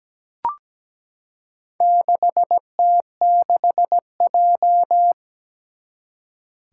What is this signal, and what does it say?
Morse code, heard as a keyed tone, about 17 wpm, 700 Hz. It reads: 6T6J
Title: Four Challenging Letter-Number Combinations 17wpm